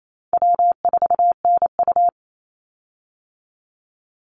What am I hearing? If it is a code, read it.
Morse code, heard as a keyed tone, about 28 wpm, 700 Hz. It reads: W4NU
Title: Easy Call Signs - 28wpm